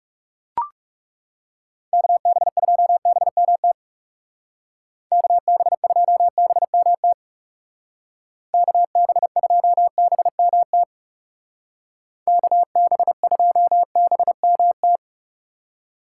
Morse code, heard as a keyed tone, about 37 wpm, 700 Hz. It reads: KB2BMT KB2BMT KB2BMT KB2BMT
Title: Speed Racing - USA 2x3 Call Signs 30wpm